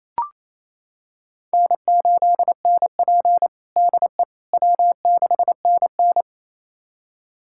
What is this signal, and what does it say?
Morse code, heard as a keyed tone, about 28 wpm, 700 Hz. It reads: N8NP DE W6NN
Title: USA Callsign DE Callsign - 4 Characters - 28wpm